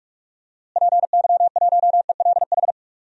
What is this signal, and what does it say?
Morse code, heard as a keyed tone, about 46 wpm, 700 Hz. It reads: PY1ELH
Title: DXCC-100 Call Signs - 6 Characters - Repeated 3x - 45wpm